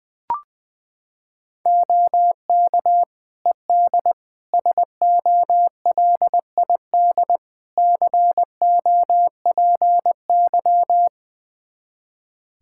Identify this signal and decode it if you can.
Morse code, heard as a keyed tone, about 20 wpm, 700 Hz. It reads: OK ED SOLID COPY
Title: K1USN SST QSO Practice 20wpm